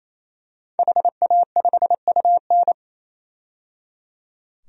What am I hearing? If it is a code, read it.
Morse code, heard as a keyed tone, about 28 wpm, 700 Hz. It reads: HA5UN
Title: DXCC-100 Call Signs - 5 Characters - 28wpm